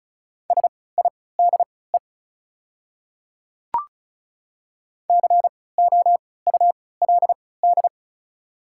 Morse code, heard as a keyed tone, about 34 wpm, spaced as for 20 wpm, 690 Hz. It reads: SIDE COULD
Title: 3x Character Spacing - Top 200 Words - 35wpm